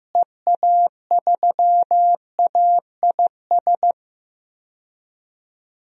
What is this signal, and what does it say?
Morse code, heard as a keyed tone, about 15 wpm, 700 Hz. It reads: EA3AIS
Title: DXCC-100 Call Signs - 6 Characters - Repeated 2x - 15wpm